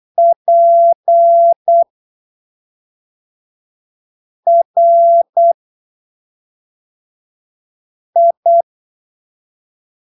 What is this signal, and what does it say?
Morse code, heard as a keyed tone, about 8 wpm, spaced as for 3 wpm, 680 Hz. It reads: PRI
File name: arquivos / texto / texto 14 08ppm